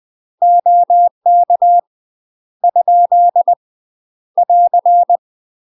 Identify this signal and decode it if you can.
Morse code, heard as a keyed tone, about 20 wpm, 700 Hz. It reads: OK ? <AR>
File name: UK MASTER QSOs TRAINING / 20WPM / UK MASTER QSOs 0049